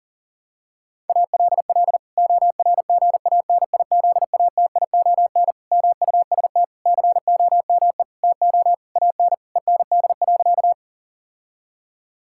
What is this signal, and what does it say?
Morse code, heard as a keyed tone, about 37 wpm, 700 Hz. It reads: ALL ORGANIZATION MUST COME TO AN END.
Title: Sentences from Top 1000 Words - Review Part 5 - 40wpm